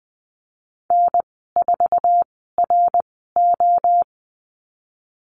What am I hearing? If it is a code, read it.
Morse code, heard as a keyed tone, about 20 wpm, spaced as for 15 wpm, 700 Hz. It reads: N4RO
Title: 069 Lesson 12g - Letter d - Call Signs 20wpm